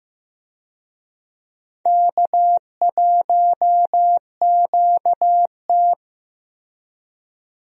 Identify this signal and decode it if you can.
Morse code, heard as a keyed tone, about 15 wpm, 700 Hz. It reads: K1QT
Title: USA Call Signs - 4 Characters - Repeated 2x - 15wpm